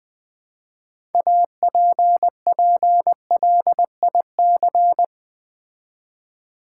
Morse code, heard as a keyed tone, about 20 wpm, 700 Hz. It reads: APPLIC
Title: Build A Word - 3 Letters At A Time - 9-Letter Words - 20wpm